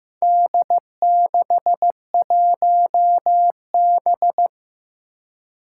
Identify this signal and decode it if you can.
Morse code, heard as a keyed tone, about 15 wpm, 700 Hz. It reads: D61B
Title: Four Challenging Letter-Number Combinations 15wpm